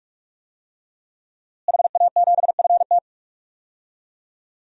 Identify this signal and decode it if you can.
Morse code, heard as a keyed tone, about 46 wpm, 700 Hz. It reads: HA7FT
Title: DXCC-100 Call Signs - 5 Characters - Repeated 3x - 45wpm